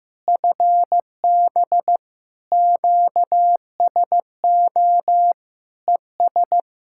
Morse code, heard as a keyed tone, about 15 wpm, 700 Hz. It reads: FB QSO ES
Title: Practice QSO - Exchange 4 - Ending - 15wpm